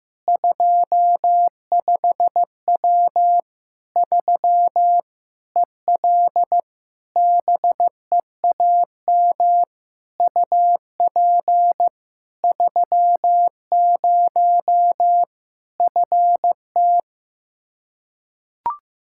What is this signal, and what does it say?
Morse code, heard as a keyed tone, about 16 wpm, 700 Hz. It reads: 25W 3 EL BEAM UP 30 FT
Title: Practice QSO - Exchange 2 - Info - 15wpm